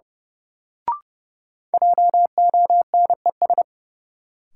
Morse code, heard as a keyed tone, about 31 wpm, 700 Hz. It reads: JONES
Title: Sets of 5 Words - New Word Drill - 30wpm